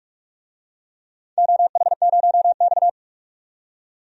Morse code, heard as a keyed tone, about 46 wpm, 690 Hz. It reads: OH0X
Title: DXCC-100 Call Signs - 4 Characters - 45wpm